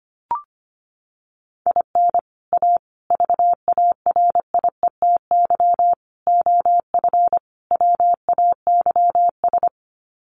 Morse code, heard as a keyed tone, about 25 wpm, 700 Hz. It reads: IN A VARIETY OF WAYS
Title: Sets of 5 Words Encore 25wpm